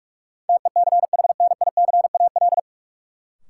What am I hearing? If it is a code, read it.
Morse code, heard as a keyed tone, about 46 wpm, 700 Hz. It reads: TECHNICAL
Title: Sentences from Top 1100 Words - New Word Drill Alt - 45wpm